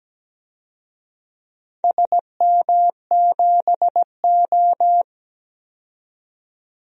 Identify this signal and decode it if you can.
Morse code, heard as a keyed tone, about 17 wpm, 700 Hz. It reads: SM7O
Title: DXCC-100 Call Signs - 4 Characters - 17wpm